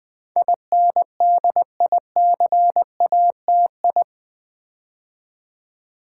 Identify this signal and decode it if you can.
Morse code, heard as a keyed tone, about 20 wpm, 700 Hz. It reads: INDICATI
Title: Build A Word - 4 Letters At A Time - 10-Letter Words - 20wpm